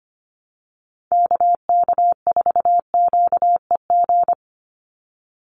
Morse code, heard as a keyed tone, about 25 wpm, 700 Hz. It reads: KK4QEG
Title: USA Call Signs 25wpm